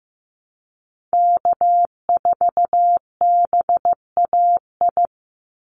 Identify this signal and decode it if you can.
Morse code, heard as a keyed tone, about 15 wpm, 700 Hz. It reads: K4BAI
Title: Call Signs Repeated 3x - 15wpm